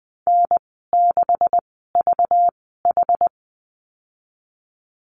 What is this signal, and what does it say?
Morse code, heard as a keyed tone, about 20 wpm, spaced as for 14 wpm, 700 Hz. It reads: N6VH
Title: 193 Lesson 31g - Letter v - Call Signs 20wpm